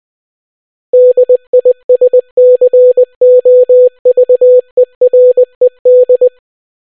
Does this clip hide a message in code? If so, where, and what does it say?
Yes: Morse code from 0.9 to 6.3 s, heard as a keyed tone, about 20 wpm, 500 Hz. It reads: DISCOVERED